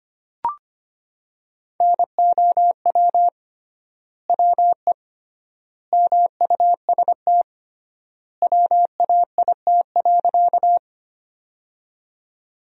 Morse code, heard as a keyed tone, about 25 wpm, 700 Hz. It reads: NOW WE MUST WAIT.
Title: 3x Word-Spacing - Sentences from Top 500 Words - 25wpm